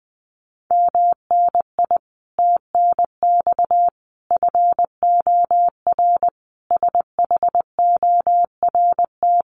Morse code, heard as a keyed tone, about 20 wpm, 700 Hz. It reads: MNI TNX FOR SHORT
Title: Common QSO Phrases 20wpm